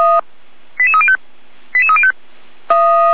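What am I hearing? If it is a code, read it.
Five-tone selective call (ZVEI), heard as quick runs of five tones, tones of about 80 ms. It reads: 88305 88305
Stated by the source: ZVEI five-tone selective call received by SDR, from the audio server of feuerwehr-orlamuende.org (fire brigade Orlamünde).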